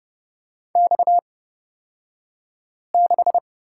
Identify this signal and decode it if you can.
Morse code, heard as a keyed tone, about 30 wpm, spaced as for 5 wpm, 700 Hz. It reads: X6
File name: Lesson 37d - Letter x - Two Characters 30wpm